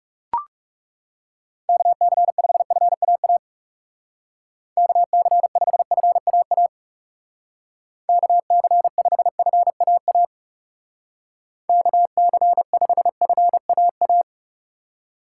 Morse code, heard as a keyed tone, about 38 wpm, 700 Hz. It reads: KC5FAA KC5FAA KC5FAA KC5FAA